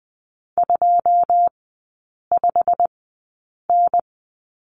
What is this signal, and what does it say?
Morse code, heard as a keyed tone, about 20 wpm, spaced as for 8 wpm, 690 Hz. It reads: 25N
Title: 191 Lesson 31e - Letter v - Three Characters 20wpm